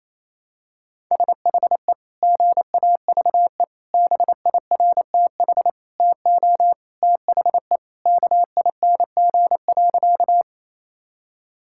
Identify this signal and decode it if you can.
Morse code, heard as a keyed tone, about 28 wpm, 700 Hz. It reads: SHE GAVE BIRTH TO THE KING.
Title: Sentences from Top 1000 Words - Review Part 5 - 28wpm